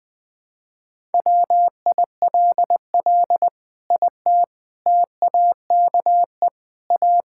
Morse code, heard as a keyed tone, about 20 wpm, 700 Hz. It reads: WILL IT TAKE A
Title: Sentences from Top 1000 Words - Review Part 6 - 20wpm